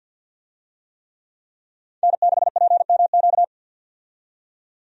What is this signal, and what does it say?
Morse code, heard as a keyed tone, about 51 wpm, 700 Hz. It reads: N6PMQ
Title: USA Call Signs - 5 Characters - 50wpm